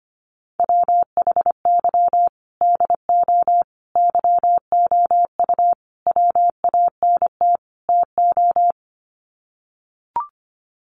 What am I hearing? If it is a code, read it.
Morse code, heard as a keyed tone, about 25 wpm, 700 Hz. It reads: WHY DO YOU WANT TO
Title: Sets of 5 Words Encore2 25wpm